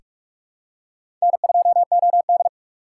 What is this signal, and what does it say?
Morse code, heard as a keyed tone, about 43 wpm, 700 Hz. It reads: N2OD
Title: USA Call Signs - 4 Characters - 4x Character Spacing - 45wpm